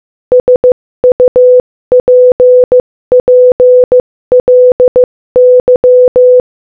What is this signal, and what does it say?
Morse code, heard as a keyed tone, about 15 wpm, 500 Hz. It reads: SUPPLY
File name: Morse-Code-Audio-Speed-15-8